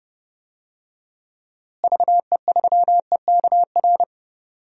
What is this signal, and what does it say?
Morse code, heard as a keyed tone, about 30 wpm, 700 Hz. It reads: VE3EKR
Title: DXCC-100 Call Signs - 6 Characters - Repeated 3x - 30wpm